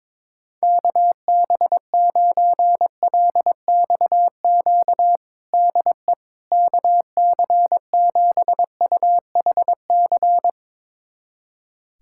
Morse code, heard as a keyed tone, about 22 wpm, 700 Hz. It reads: KB9LXQ DE KC7UHC
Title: USA Callsign DE Callsign - 6 Characters - 22wpm